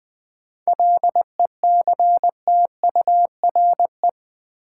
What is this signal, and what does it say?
Morse code, heard as a keyed tone, about 20 wpm, 700 Hz. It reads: LECTURE
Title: Sentences from Top 1600 Words - New Word Drill - 20wpm